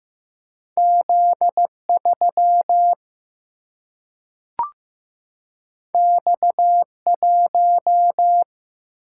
Morse code, heard as a keyed tone, about 15 wpm, 700 Hz. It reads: Z 3 X 1
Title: Single-Letter Single-Number 15wpm